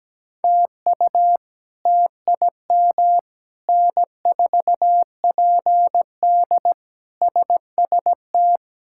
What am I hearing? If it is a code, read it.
Morse code, heard as a keyed tone, about 17 wpm, 700 Hz. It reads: TU TIM N4PD SST
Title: K1USN SST QSO Practice - 17wpm